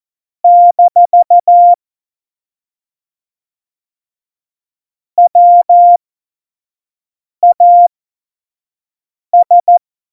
Morse code, heard as a keyed tone, about 14 wpm, spaced as for 5 wpm, 700 Hz. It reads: - WAS
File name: Morsekurs / Witze / Witz 21. Mai 2019, 25 F70